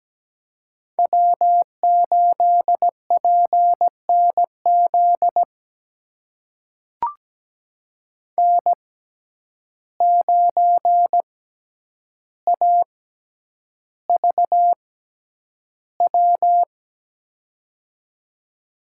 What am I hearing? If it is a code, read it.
Morse code, heard as a keyed tone, about 17 wpm, spaced as for 9 wpm, 700 Hz. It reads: W8PNZ N9AVW
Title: USA Call Signs - 5 Characters - 6x Character Spacing - 17wpm